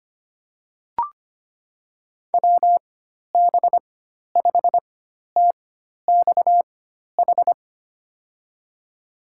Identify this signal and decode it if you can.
Morse code, heard as a keyed tone, about 25 wpm, spaced as for 12 wpm, 700 Hz. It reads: WB5TXH
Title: USA Call Signs - 6 Characters - 4x Character Spacing - 25wpm